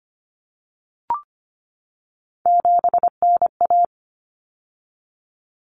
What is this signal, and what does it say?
Morse code, heard as a keyed tone, about 25 wpm, 700 Hz. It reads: 7NA
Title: Common Cut Numbers - 3-Digit 25wpm